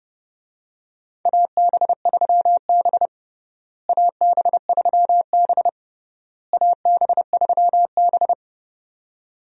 Morse code, heard as a keyed tone, about 30 wpm, 700 Hz. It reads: AB3B AB3B AB3B
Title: USA Call Signs - 4 Characters - Repeated 3x - 30wpm